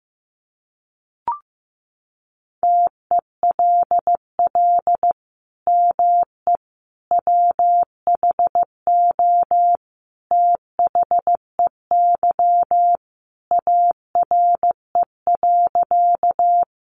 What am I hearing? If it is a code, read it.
Morse code, heard as a keyed tone, about 15 wpm, 700 Hz. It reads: TELL ME WHO THEY ARE.
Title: Sentences from Top 700 Words 15wpm